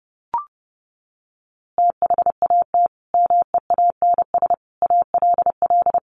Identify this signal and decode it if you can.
Morse code, heard as a keyed tone, about 30 wpm, 700 Hz. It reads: THAT MEANS ALL
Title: A Sound Of Thunder 30wpm